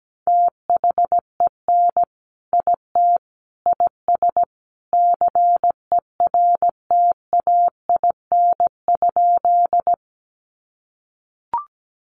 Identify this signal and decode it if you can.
Morse code, heard as a keyed tone, about 17 wpm, 700 Hz. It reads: THEN IT IS CERTAIN?
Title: Sentences from Top 500 Words 17wpm